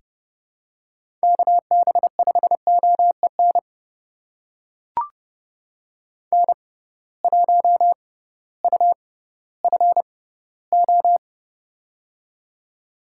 Morse code, heard as a keyed tone, about 30 wpm, spaced as for 16 wpm, 700 Hz. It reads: KB5OEN N1UFO
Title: USA Call Signs - 6x Character Spacing - 30wpm